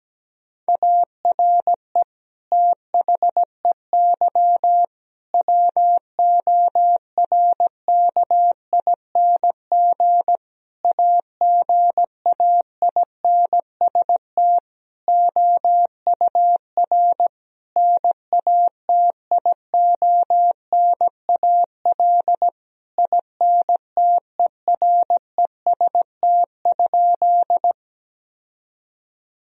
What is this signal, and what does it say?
Morse code, heard as a keyed tone, about 17 wpm, 700 Hz. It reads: ARE THEY WORKING AGAINST OUR NATIONAL INTEREST?
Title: Sentences from Top 1000 Words - Review Part 4 - 17wpm